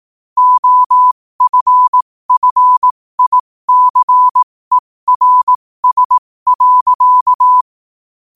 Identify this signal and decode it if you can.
Morse code, heard as a keyed tone, about 18 wpm, spaced as for 15 wpm, 1000 Hz. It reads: OFFICERS.
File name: qotdfiles / 15wpm